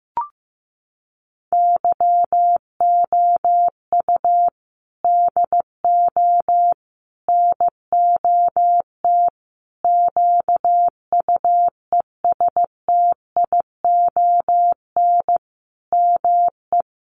Morse code, heard as a keyed tone, about 15 wpm, 700 Hz. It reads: YOU DO NOT QUESTION ME
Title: Sentences from Top 1100 Words 15wpm